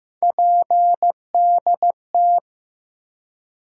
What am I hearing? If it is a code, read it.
Morse code, heard as a keyed tone, about 15 wpm, 700 Hz. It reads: PDT
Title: USA Call Sign Postfixes - 3 Letters 15wpm